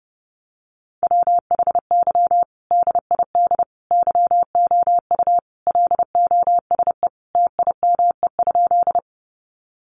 Morse code, heard as a keyed tone, about 30 wpm, 700 Hz. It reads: WHY DID YOU LOSE TIME?
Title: Sentences from Top 1100 Words 30wpm